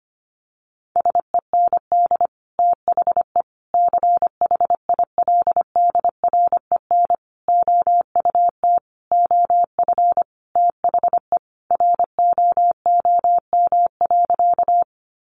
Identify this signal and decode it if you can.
Morse code, heard as a keyed tone, about 25 wpm, 700 Hz. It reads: SEND THE CHILDREN OUT OF THE ROOM.
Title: Sentences from Top 1500 Words 25wpm